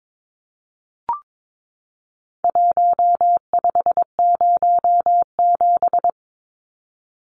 Morse code, heard as a keyed tone, about 22 wpm, 700 Hz. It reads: 1507
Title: Numbers - Contest Style - 4-Digit 22wpm